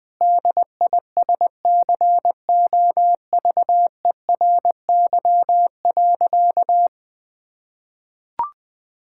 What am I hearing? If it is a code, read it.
Morse code, heard as a keyed tone, about 20 wpm, 700 Hz. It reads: DISCOVERY.
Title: Sentences with 8-Word Phrases - 20wpm